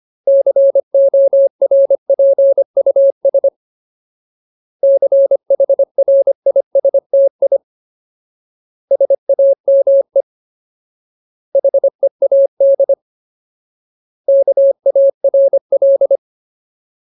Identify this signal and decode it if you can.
Morse code, heard as a keyed tone, about 25 wpm, 550 Hz. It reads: CORPUS CHRISTI SAME HEAD KARL
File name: Ultimate Word List / 25WPM Ultimate